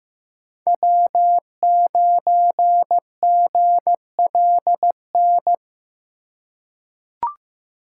Morse code, heard as a keyed tone, about 15 wpm, 700 Hz. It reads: W9GLN
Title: USA Call Signs - 5 Characters - 4x Character Spacing - 15wpm